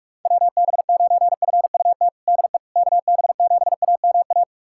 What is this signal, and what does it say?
Morse code, heard as a keyed tone, about 41 wpm, 690 Hz. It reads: WB9FUT DE KB7AMA
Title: USA Callsign DE Callsign - 6 Characters - 45wpm